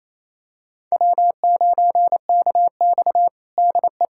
Morse code, heard as a keyed tone, about 28 wpm, 700 Hz. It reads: W9KX DE
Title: USA Callsign DE Callsign - 4 Characters - 28wpm